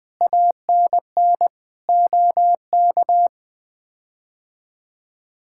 Morse code, heard as a keyed tone, about 20 wpm, 700 Hz. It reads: ANN OK
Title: K1USN SST QSO Practice - 20wpm